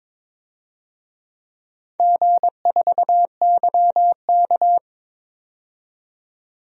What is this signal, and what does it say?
Morse code, heard as a keyed tone, about 21 wpm, 700 Hz. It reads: G4YK
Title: DXCC-100 Call Signs - 4 Characters - 22wpm